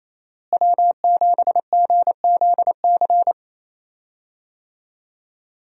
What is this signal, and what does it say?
Morse code, heard as a keyed tone, about 28 wpm, 700 Hz. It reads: W7GZC
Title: USA Call Signs - 5 Characters - Repeated 2x - 28wpm